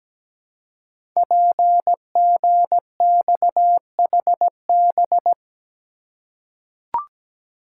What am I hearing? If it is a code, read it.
Morse code, heard as a keyed tone, about 17 wpm, 700 Hz. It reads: PGXHB
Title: Code Groups - 5 Letters - 17wpm